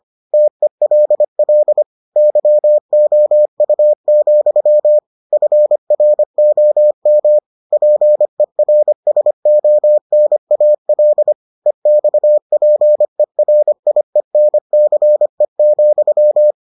Morse code, heard as a keyed tone, about 25 wpm, 600 Hz. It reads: TELL YOU, FROM PERSONAL EXPERIENCE,